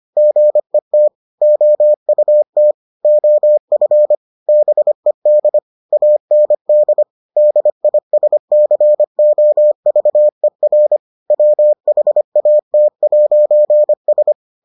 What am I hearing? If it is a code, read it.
Morse code, heard as a keyed tone, about 25 wpm, 600 Hz. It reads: GET OUT OF BED AND DISCOVER WHAT'S